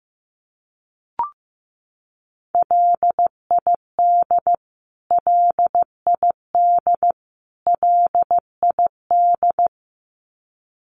Morse code, heard as a keyed tone, about 15 wpm, 700 Hz. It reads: LID LID LID
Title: Abbreviations Repeated 3x - 15wpm